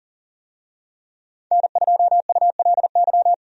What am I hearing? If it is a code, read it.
Morse code, heard as a keyed tone, about 37 wpm, 700 Hz. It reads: N2ULY
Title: USA Call Signs - 5 Characters - 40wpm